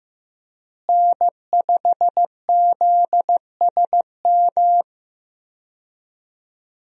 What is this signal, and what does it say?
Morse code, heard as a keyed tone, about 15 wpm, 700 Hz. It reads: N5ZSM
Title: USA Call Signs - 5 Characters - Repeated 3x - 15wpm